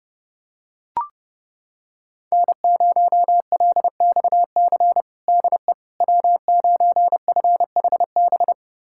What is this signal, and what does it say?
Morse code, heard as a keyed tone, about 30 wpm, 700 Hz. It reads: N0LXC DE W9FHB
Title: USA Callsign DE Callsign - 5 Characters - 30wpm